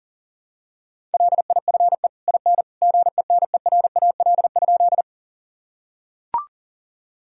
Morse code, heard as a keyed tone, about 40 wpm, 700 Hz. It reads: LIFE IN GENERAL?
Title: Sentences from Top 1000 Words - Review Part 3 - 40wpm